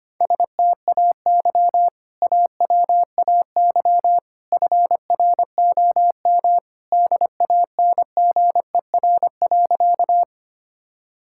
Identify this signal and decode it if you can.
Morse code, heard as a keyed tone, about 25 wpm, 690 Hz. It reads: STAY AWAY FROM DANGER.
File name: practice / 25wpm / Sentences with 2-Word Phrases - 25wpm